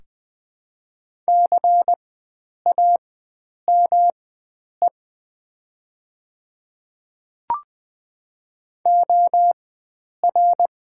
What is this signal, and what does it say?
Morse code, heard as a keyed tone, about 20 wpm, spaced as for 9 wpm, 700 Hz. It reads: CAME OR
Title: 4x Character Spacing - Top 500 Words - 20wpm